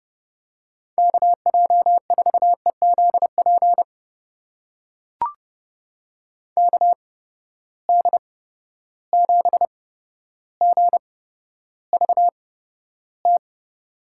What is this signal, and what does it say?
Morse code, heard as a keyed tone, about 30 wpm, spaced as for 13 wpm, 700 Hz. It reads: KJ4EZP KD7GVT
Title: USA Call Signs - 6 Characters - 8x Character Spacing - 30wpm